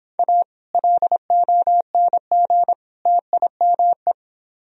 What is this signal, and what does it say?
Morse code, heard as a keyed tone, about 29 wpm, 700 Hz. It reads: A LONG TIME